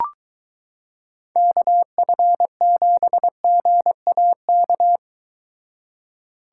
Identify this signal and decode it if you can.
Morse code, heard as a keyed tone, about 22 wpm, 700 Hz. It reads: KF7GAK